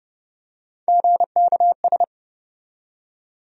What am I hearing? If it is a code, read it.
Morse code, heard as a keyed tone, about 30 wpm, 700 Hz. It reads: GKS
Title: USA Call Sign Postfixes - 3 Letters 30wpm